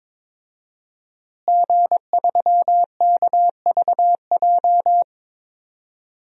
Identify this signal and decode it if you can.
Morse code, heard as a keyed tone, about 22 wpm, 700 Hz. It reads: G3KVJ
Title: DXCC-100 Call Signs - 5 Characters - Repeated 2x - 22wpm